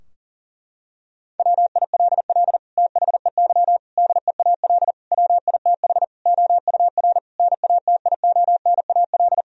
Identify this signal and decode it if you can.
Morse code, heard as a keyed tone, about 40 wpm, 700 Hz. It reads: WILL THEY DEAL WITH OUR NATIONAL
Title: Sentences from Top 1000 Words - Review Part 4 - 40wpm